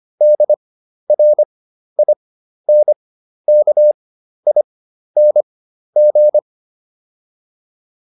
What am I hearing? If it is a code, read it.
Morse code, heard as a keyed tone, about 25 wpm, spaced as for 12 wpm, 600 Hz. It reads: DRINKING